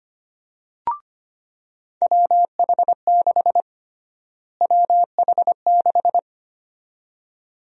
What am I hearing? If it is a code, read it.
Morse code, heard as a keyed tone, about 25 wpm, 700 Hz. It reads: WH6 WH6
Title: USA Call Sign Prefixes - Repeated 2x - 25wpm